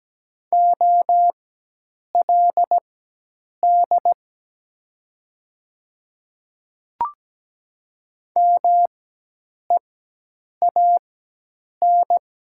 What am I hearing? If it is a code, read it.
Morse code, heard as a keyed tone, about 17 wpm, spaced as for 8 wpm, 700 Hz. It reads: OLD MEAN
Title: 4x Character Spacing - Top 300 Words - 17wpm